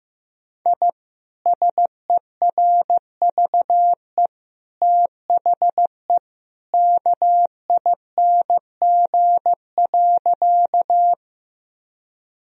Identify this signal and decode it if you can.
Morse code, heard as a keyed tone, about 16 wpm, 700 Hz. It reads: I SERVE THE KING.
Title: Sentences from Top 1000 Words - Review Part 2 - 15wpm